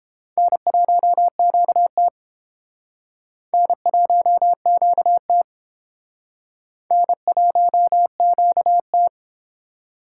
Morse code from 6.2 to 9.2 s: N1QT

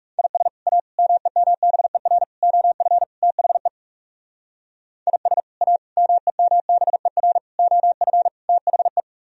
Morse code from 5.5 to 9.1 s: A MEMBER OF THE

